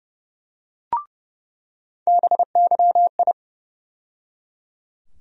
Morse code, heard as a keyed tone, about 29 wpm, 700 Hz. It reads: BYI
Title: USA Call Sign Postfixes - 3 Letters 30wpm